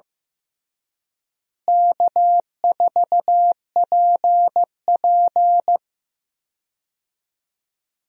Morse code, heard as a keyed tone, about 16 wpm, 700 Hz. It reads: K4PP
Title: USA Call Signs - 4 Characters - Repeated 3x - 15wpm